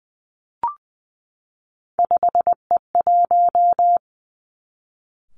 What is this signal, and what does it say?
Morse code, heard as a keyed tone, about 20 wpm, 700 Hz. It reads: 5E1
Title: Three Challenging Letter-Number Combinations 20wpm